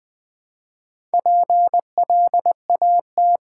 Morse code, heard as a keyed tone, about 20 wpm, 700 Hz. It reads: PLAT
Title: Build A Word - 2 Letters At A Time - 8-Letter Words - 20wpm